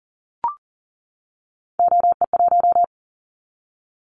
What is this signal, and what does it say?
Morse code, heard as a keyed tone, about 42 wpm, 700 Hz. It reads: OE1
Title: Callsign Prefixes for Top 50 DXCC - 40wpm